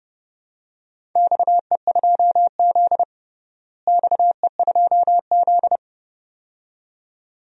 Morse code, heard as a keyed tone, about 30 wpm, 700 Hz. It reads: XE2Z XE2Z
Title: DXCC-100 Call Signs - 4 Characters - Repeated 2x - 30wpm